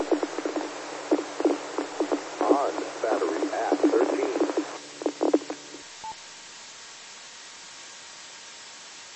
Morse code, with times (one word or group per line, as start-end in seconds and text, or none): none